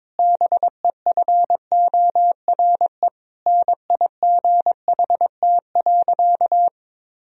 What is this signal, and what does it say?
Morse code, heard as a keyed tone, about 21 wpm, 700 Hz. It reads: BEFORE NIGHT.
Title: Sentences from Top 1000 Words - Review Part 3 - 22wpm